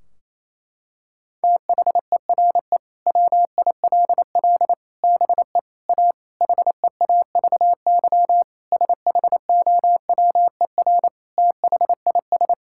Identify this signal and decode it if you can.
Morse code, heard as a keyed tone, about 28 wpm, 700 Hz. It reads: THERE WILL BE A HEAVY SHOWER THIS
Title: Sentences from Top 1600 Words 28wpm